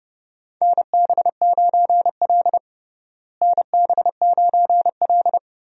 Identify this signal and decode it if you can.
Morse code, heard as a keyed tone, about 30 wpm, 700 Hz. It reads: NB9L NB9L
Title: USA Call Signs - 4 Characters - Repeated 2x - 30wpm